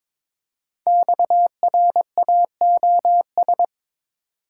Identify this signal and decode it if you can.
Morse code, heard as a keyed tone, about 22 wpm, 700 Hz. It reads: XRAOS
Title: Code Groups - 5 Letters - 22wpm